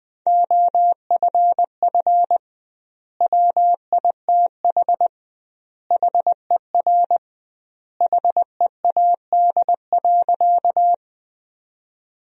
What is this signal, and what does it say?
Morse code, heard as a keyed tone, about 21 wpm, 700 Hz. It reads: OFF WITH HER HEAD.
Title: 2x Word-Spacing - Sentences from Top 300 Words - 20wpm